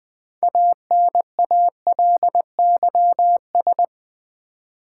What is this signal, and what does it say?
Morse code, heard as a keyed tone, about 20 wpm, 700 Hz. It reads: ANALYS
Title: Build A Word - 2 Letters At A Time - 8-Letter Words - 20wpm